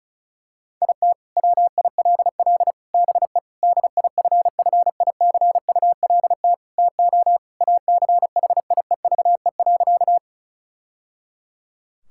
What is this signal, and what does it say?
Morse code, heard as a keyed tone, about 34 wpm, 700 Hz. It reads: IT WILL BE DIFFICULT TO ACHIEVE.
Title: Sentences from Top 1000 Words - Review Part 6 - 35wpm